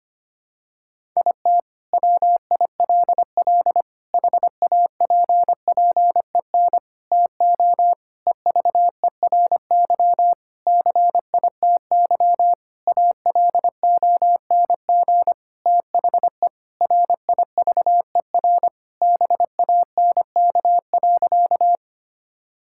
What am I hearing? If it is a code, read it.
Morse code, heard as a keyed tone, about 25 wpm, 700 Hz. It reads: IT WILL HAPPEN TO EVERY CITY ALONG THE RIVER BANK.
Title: Sentences from Top 1000 Words - Review Part 6 - 25wpm